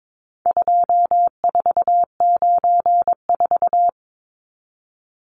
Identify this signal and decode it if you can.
Morse code, heard as a keyed tone, about 22 wpm, 700 Hz. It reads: 2494
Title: Numbers - 4-Digit 22wpm